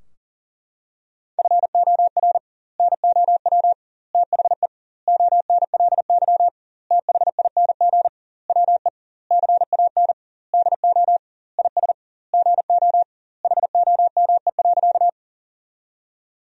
Morse code, heard as a keyed tone, about 37 wpm, 700 Hz. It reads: FOR NOW THE ONLY THING WE CAN DO IS GO HOME.
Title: 2x Word-Spacing - Sentences from Top 200 Words - 40wpm